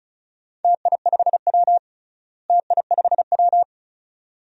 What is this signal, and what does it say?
Morse code, heard as a keyed tone, about 34 wpm, 700 Hz. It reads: TI5W TI5W
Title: DXCC-100 Call Signs - 4 Characters - Repeated 2x - 35wpm